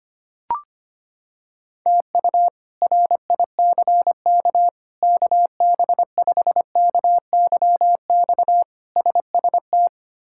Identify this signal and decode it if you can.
Morse code, heard as a keyed tone, about 25 wpm, 700 Hz. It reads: TU RICK KB5KYX SST
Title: K1USN SST QSO Practice - 25wpm